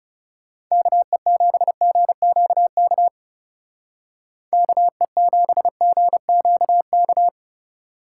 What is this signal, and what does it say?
Morse code, heard as a keyed tone, about 35 wpm, 700 Hz. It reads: KE7GQK KE7GQK
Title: Speed Racing - USA Call Signs 30wpm